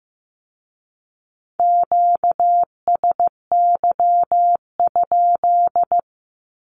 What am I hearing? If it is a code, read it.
Morse code, heard as a keyed tone, about 16 wpm, 700 Hz. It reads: QSY?
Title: Q-Codes with Spoken Meaning - 15wpm